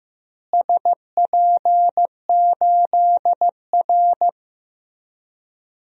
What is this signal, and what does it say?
Morse code, heard as a keyed tone, about 15 wpm, 700 Hz. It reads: SP8R
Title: DXCC-100 Call Signs - 4 Characters - Repeated 3x - 15wpm